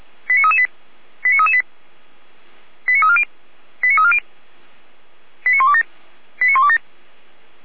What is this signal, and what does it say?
Five-tone selective call (ZVEI), heard as quick runs of five tones, tones of about 70 ms. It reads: 89338 89338 89340 89340 89127 89127